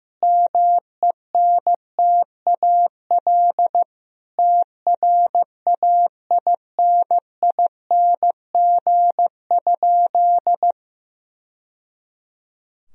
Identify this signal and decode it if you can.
Morse code, heard as a keyed tone, about 15 wpm, 700 Hz. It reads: MENTAL TRAINING?
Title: Sentences from Top 1000 Words - Review Part 4 - 15wpm